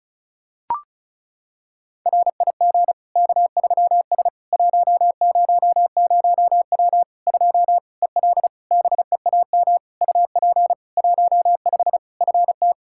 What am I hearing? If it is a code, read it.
Morse code, heard as a keyed tone, about 36 wpm, 700 Hz. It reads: RIG K3S 100W 2 EL BEAM UP 15 FT
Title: Practice QSO - Exchange 2 - Info - 35wpm